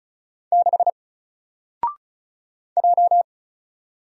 Morse code, heard as a keyed tone, about 35 wpm, 700 Hz. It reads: 6 J
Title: Single Character (Letter:Number:Punctuation:Prosign) 35wpm